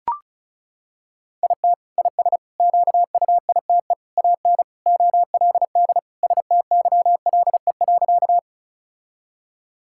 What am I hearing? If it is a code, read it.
Morse code, heard as a keyed tone, about 36 wpm, 700 Hz. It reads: IT IS QUITE AN OLD STYLE.
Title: Sentences from Top 1000 Words - Review Part 5 - 35wpm